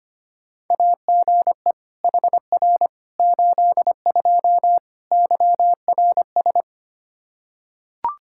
Morse code, heard as a keyed tone, about 25 wpm, 700 Hz. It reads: AGE HR 82 YRS
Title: K1USN SST QSO Practice 25wpm